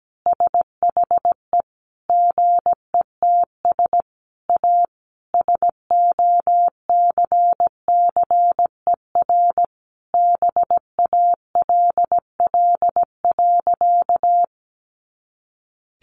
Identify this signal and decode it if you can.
Morse code, heard as a keyed tone, about 17 wpm, 700 Hz. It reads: SHE GETS A SOCCER BALL.
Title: Easy Phrases 17wpm